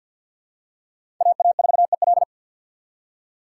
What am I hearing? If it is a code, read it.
Morse code, heard as a keyed tone, about 53 wpm, 700 Hz. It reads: AA4EL
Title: USA Call Signs - 3x Character Spacing - 50wpm